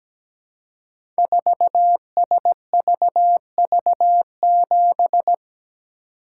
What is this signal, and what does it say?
Morse code, heard as a keyed tone, about 17 wpm, 700 Hz. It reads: 4SVV7
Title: Code Groups - 5 Letter:Numbers - 17wpm